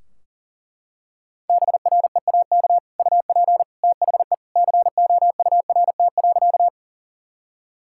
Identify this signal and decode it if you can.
Morse code, heard as a keyed tone, about 40 wpm, 700 Hz. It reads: BREAK UP THE COURT.
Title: Sentences from Top 1600 Words 40wpm